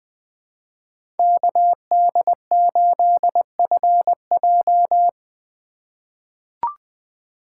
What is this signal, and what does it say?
Morse code, heard as a keyed tone, about 20 wpm, 700 Hz. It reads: KD8FJ
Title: USA Call Signs - 4x Character Spacing - 20wpm